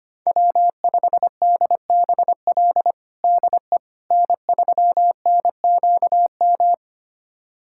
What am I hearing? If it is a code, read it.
Morse code, heard as a keyed tone, about 25 wpm, 700 Hz. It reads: W5DBL DE N3NQM
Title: USA Callsign DE Callsign - 5 Characters - 25wpm